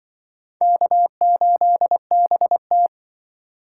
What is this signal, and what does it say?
Morse code, heard as a keyed tone, about 24 wpm, 700 Hz. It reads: K8BT
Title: Speed Racing - USA 1x2 Call Signs 20wpm